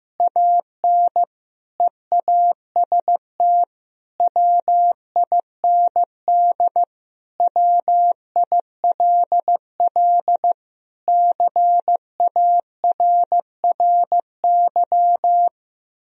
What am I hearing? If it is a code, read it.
Morse code, heard as a keyed tone, about 15 wpm, 700 Hz. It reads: AN EAST WIND WILL CARRY
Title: Sentences from Top 1000 Words - Review Part 3 - 15wpm